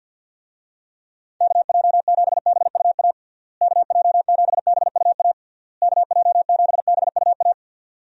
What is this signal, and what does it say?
Morse code, heard as a keyed tone, about 50 wpm, 700 Hz. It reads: KJ7BUA KJ7BUA KJ7BUA
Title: USA Call Signs - 6 Characters - Repeated 3x - 50wpm